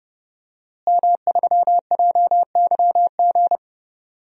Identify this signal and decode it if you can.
Morse code, heard as a keyed tone, about 32 wpm, 700 Hz. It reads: M3JYG
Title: DXCC-100 Call Signs - 5 Characters - Repeated 3x - 30wpm